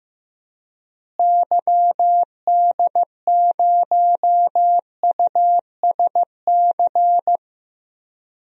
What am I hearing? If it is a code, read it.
Morse code, heard as a keyed tone, about 16 wpm, 700 Hz. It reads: YD0USC
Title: DXCC-100 Call Signs - 6 Characters - Repeated 2x - 15wpm